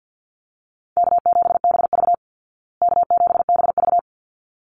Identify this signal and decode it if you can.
Morse code, heard as a keyed tone, about 50 wpm, 700 Hz. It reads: K7BV K7BV
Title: Call Signs Repeated 3x - 50wpm